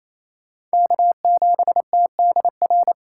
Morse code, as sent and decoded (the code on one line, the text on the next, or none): -.- --... - -.. .-.
K7TDR